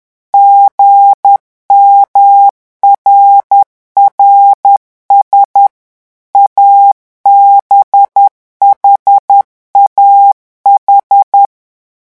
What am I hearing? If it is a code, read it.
Morse code, heard as a keyed tone, about 10 wpm, 800 Hz. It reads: GMRRS ABHAH
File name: morse mp3 / revision5-vitesse 10 mots